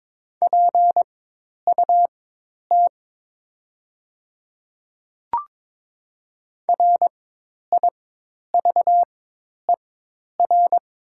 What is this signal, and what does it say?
Morse code, heard as a keyed tone, about 22 wpm, spaced as for 10 wpm, 700 Hz. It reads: PUT RIVER
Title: 4x Character Spacing - Top 300 Words - 22wpm